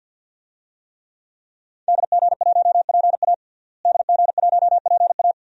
Morse code, heard as a keyed tone, about 52 wpm, 700 Hz. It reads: DG1PA DG1PA
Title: DXCC-100 Call Signs - 5 Characters - Repeated 2x - 50wpm